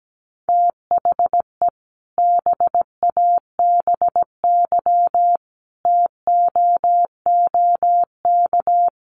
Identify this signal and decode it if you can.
Morse code, heard as a keyed tone, about 17 wpm, 700 Hz. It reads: THE BABY TOOK
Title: Sentences from Top 1300 Words 17wpm